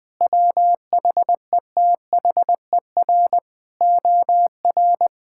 Morse code, heard as a keyed tone, about 20 wpm, 700 Hz. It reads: WHETHER OR
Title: Sentences with 8-Word Phrases - 20wpm